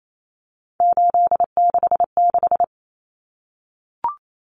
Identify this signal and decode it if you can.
Morse code, heard as a keyed tone, about 28 wpm, 700 Hz. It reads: 866
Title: Common Cut Numbers - 3-Digit 28wpm